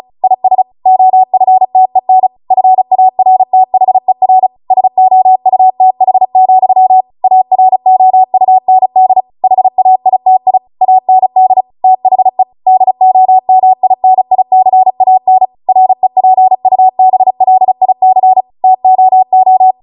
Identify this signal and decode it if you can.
Morse code, heard as a keyed tone, about 36 wpm, 750 Hz. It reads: IS OFTEN FARTHER SOUTH, AROUND HAITI AND THE DOMINICAN REPUBLIC TOO